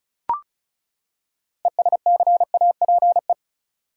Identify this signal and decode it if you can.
Morse code, heard as a keyed tone, about 36 wpm, 700 Hz. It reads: ESCAPE
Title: Six Letter Words - 35wpm